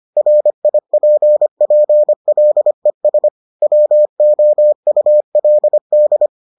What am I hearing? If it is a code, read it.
Morse code, heard as a keyed tone, about 25 wpm, 600 Hz. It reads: RIPPLES WOULD